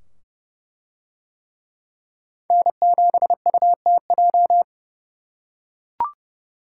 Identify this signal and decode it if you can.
Morse code, heard as a keyed tone, about 28 wpm, 700 Hz. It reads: N7UTJ